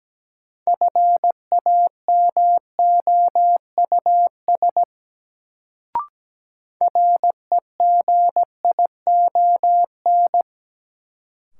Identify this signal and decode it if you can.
Morse code, heard as a keyed tone, about 17 wpm, 700 Hz. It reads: FAMOUS REGION
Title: Sentences from Top 1500 Words - New Word Drill Alt - 17wpm